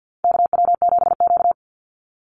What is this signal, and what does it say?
Morse code, heard as a keyed tone, about 50 wpm, 700 Hz. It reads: KW7Q
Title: Call Signs Repeated 2x - 50wpm